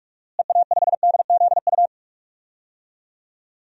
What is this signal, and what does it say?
Morse code, heard as a keyed tone, about 45 wpm, 700 Hz. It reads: EA5DZU
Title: DXCC-100 Call Signs - 6 Characters - Repeated 3x - 45wpm